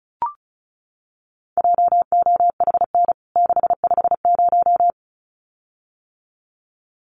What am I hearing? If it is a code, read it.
Morse code, heard as a keyed tone, about 34 wpm, 700 Hz. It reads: JOHN 650
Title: CWT Exchange 35wpm